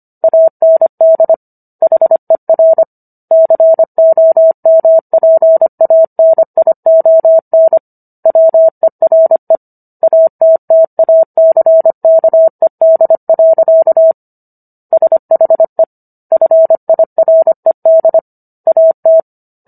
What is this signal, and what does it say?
Morse code, heard as a keyed tone, about 25 wpm, 650 Hz. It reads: AND HER COMPANION WERE ATTACKED. SHE FIRED AT